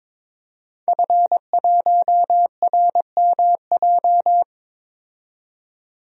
Morse code, heard as a keyed tone, about 21 wpm, 700 Hz. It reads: F1RMJ
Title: DXCC-100 Call Signs - 5 Characters - 22wpm